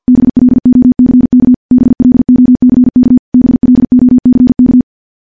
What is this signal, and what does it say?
Morse code, heard as a keyed tone, about 50 wpm, 250 Hz. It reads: BLOCK BLOCK BLOCK
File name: morse / morse-training-50WPM